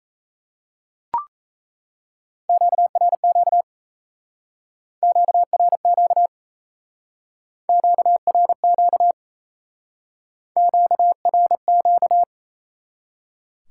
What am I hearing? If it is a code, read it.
Morse code, heard as a keyed tone, about 37 wpm, 700 Hz. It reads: QRQ QRQ QRQ QRQ